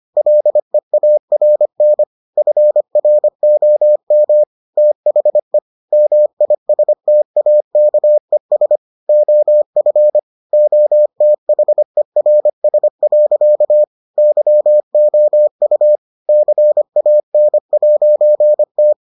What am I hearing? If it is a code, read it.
Morse code, heard as a keyed tone, about 25 wpm, 600 Hz. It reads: LEARN FROM THE MISTAKES OF OTHERS. YOU CAN'T